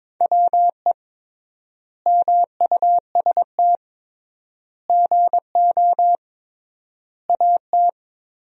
Morse code, heard as a keyed tone, about 22 wpm, 700 Hz. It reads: WE MUST GO AT
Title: 3x Word-Spacing - Sentences from Top 300 Words - 22wpm